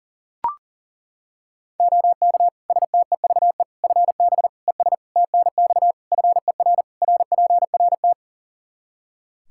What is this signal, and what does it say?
Morse code, heard as a keyed tone, about 40 wpm, 700 Hz. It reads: OK STEVE FB ES TNX FER RPRT
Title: Practice QSO - Exchange 2 - Info - 40wpm